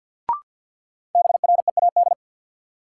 Morse code, heard as a keyed tone, about 53 wpm, 700 Hz. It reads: BREAD
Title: Sentences from Top 1600 Words - New Word Drill - 50wpm